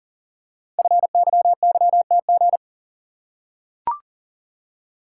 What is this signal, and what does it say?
Morse code, heard as a keyed tone, about 44 wpm, 690 Hz. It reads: FYYTG
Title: Code Groups - 5 Letters - 40wpm